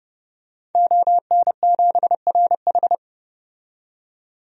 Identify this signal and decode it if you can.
Morse code, heard as a keyed tone, about 31 wpm, 700 Hz. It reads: ON7RH
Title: DXCC-100 Call Signs - 5 Characters - 30wpm